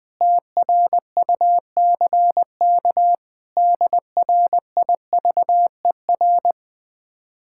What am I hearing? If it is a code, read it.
Morse code, heard as a keyed tone, about 20 wpm, 700 Hz. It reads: TRUCK DRIVER
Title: Speed-Racing Common QSO Phrases Encore - 17wpm